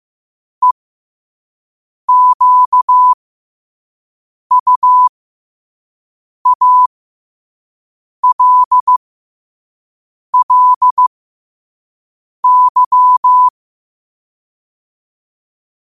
Morse code, heard as a keyed tone, about 15 wpm, spaced as for 5 wpm, 1000 Hz. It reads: EQUALLY
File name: qotdfiles / 5wpm